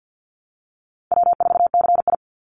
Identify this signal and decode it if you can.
Morse code, heard as a keyed tone, about 50 wpm, 700 Hz. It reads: W4CI
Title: Call Signs Repeated 2x - 50wpm